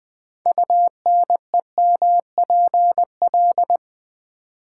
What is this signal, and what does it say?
Morse code, heard as a keyed tone, about 20 wpm, 700 Hz. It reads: UNEMPL
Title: Build A Word - 6 Letters At A Time - 12-Letter Words - 20wpm